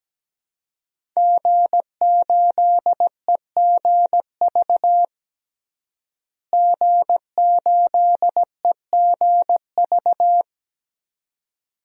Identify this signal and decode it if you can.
Morse code, heard as a keyed tone, about 17 wpm, 700 Hz. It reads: G8EGV G8EGV
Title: DXCC-100 Call Signs - 5 Characters - Repeated 2x - 17wpm